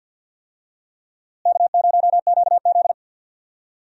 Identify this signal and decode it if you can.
Morse code, heard as a keyed tone, about 52 wpm, 700 Hz. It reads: K0QZ
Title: USA Call Signs - 4 Characters - Repeated 3x - 50wpm